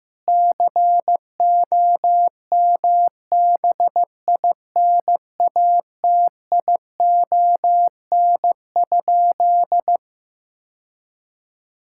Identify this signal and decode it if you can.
Morse code, heard as a keyed tone, about 15 wpm, 700 Hz. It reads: COMBINATION?
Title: Sentences from Top 1000 Words - Review Part 6 - 15wpm